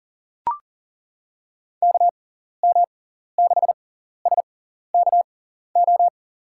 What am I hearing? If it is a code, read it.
Morse code, heard as a keyed tone, about 40 wpm, spaced as for 14 wpm, 700 Hz. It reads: KM6SKO
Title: USA Call Signs - 6x Character Spacing - 40wpm